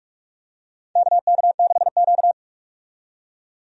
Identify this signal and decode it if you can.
Morse code, heard as a keyed tone, about 46 wpm, 700 Hz. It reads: KK6Q
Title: USA Call Signs - 4 Characters - 45wpm